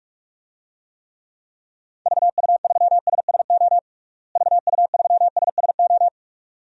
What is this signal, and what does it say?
Morse code, heard as a keyed tone, about 45 wpm, 700 Hz. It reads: VU3SSO VU3SSO
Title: DXCC-100 Call Signs - 6 Characters - Repeated 2x - 45wpm